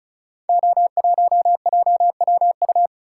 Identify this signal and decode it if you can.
Morse code, heard as a keyed tone, about 36 wpm, 690 Hz. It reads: O1JWU
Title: Code Groups - 5 Letter:Numbers - 35wpm